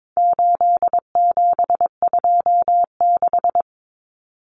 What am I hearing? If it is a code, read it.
Morse code, heard as a keyed tone, about 22 wpm, 700 Hz. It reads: 8726
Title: Common Cut Numbers - 4-Digit 22wpm